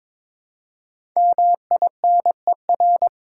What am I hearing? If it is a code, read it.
Morse code, heard as a keyed tone, about 22 wpm, 700 Hz. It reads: MINER
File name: 5 Letter Words - First 3 Letters Given - 22wpm